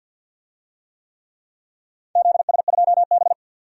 Morse code, heard as a keyed tone, about 50 wpm, 700 Hz. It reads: ZS2B
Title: DXCC-100 Call Signs - 4 Characters - 50wpm